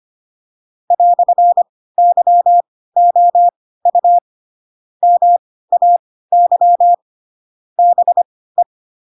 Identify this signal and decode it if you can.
Morse code, heard as a keyed tone, about 25 wpm, spaced as for 16 wpm, 700 Hz. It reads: "YOU MAY BE
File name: Pinocchio / Pinocchio ch7 10 25 ik1hge